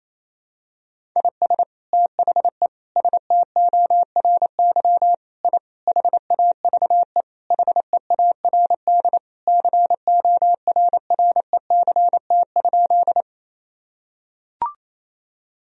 Morse code, heard as a keyed tone, about 28 wpm, 700 Hz. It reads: IS THE STORY I HAVE HEARD CORRECT?
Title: Sentences from Top 1000 Words - Review Part 2 - 28wpm